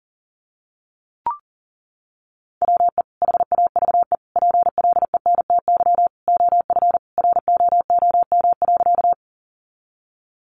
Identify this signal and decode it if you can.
Morse code, heard as a keyed tone, about 40 wpm, 700 Hz. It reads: WE HAVE PLENTY OF ROOM.
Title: Sentences from Top 1300 Words 40wpm